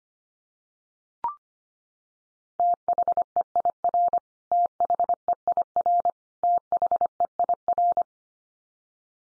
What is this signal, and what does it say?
Morse code, heard as a keyed tone, about 25 wpm, 700 Hz. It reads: THEIR THEIR THEIR
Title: Top 100 Words 25wpm x3